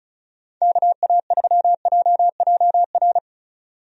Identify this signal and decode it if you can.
Morse code, heard as a keyed tone, about 34 wpm, 700 Hz. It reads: KA3JJR
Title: USA Call Signs - 6 Characters - 35wpm